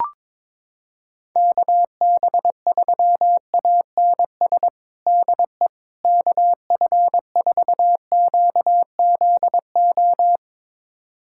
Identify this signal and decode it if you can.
Morse code, heard as a keyed tone, about 22 wpm, 700 Hz. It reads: KB3ANS DE KF4QZO
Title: USA Callsign DE Callsign - 6 Characters - 22wpm